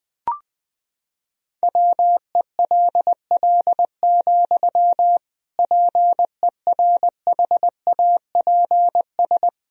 Morse code, heard as a keyed tone, about 21 wpm, 700 Hz. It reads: WELL, PERHAPS
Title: Sentences from Top 1000 Words - Review Part 2 - 20wpm